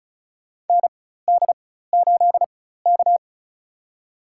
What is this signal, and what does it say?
Morse code, heard as a keyed tone, about 34 wpm, spaced as for 16 wpm, 700 Hz. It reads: ND8K
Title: USA Call Signs - 4 Characters - 4x Character Spacing - 35wpm